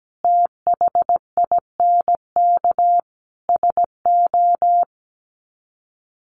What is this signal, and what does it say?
Morse code, heard as a keyed tone, about 17 wpm, 700 Hz. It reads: THINK SO
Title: Sets of 4 Words 17wpm